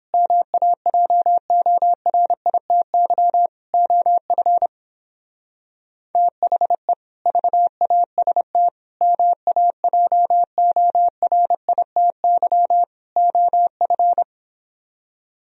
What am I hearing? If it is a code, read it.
Morse code, heard as a keyed tone, about 29 wpm, 700 Hz. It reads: MAJORITY OF THE VAST MAJORITY OF